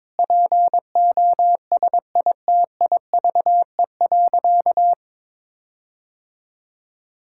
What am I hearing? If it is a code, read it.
Morse code, heard as a keyed tone, about 22 wpm, 700 Hz. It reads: POSITIVE.
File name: practice / 22wpm / Sentences with 8-Word Phrases - 22wpm